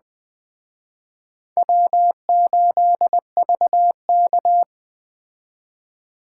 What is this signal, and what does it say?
Morse code, heard as a keyed tone, about 21 wpm, 700 Hz. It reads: W8VK
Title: Speed Racing - USA 1x2 Call Signs 17wpm